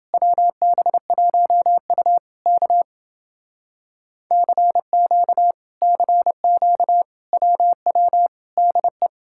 Morse code, heard as a keyed tone, about 30 wpm, 700 Hz. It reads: WB1U K CQ CQ WW DE